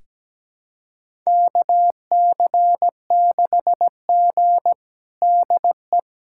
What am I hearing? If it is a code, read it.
Morse code, heard as a keyed tone, about 17 wpm, 700 Hz. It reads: KC6G DE
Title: USA Callsign DE Callsign - 4 Characters - 17wpm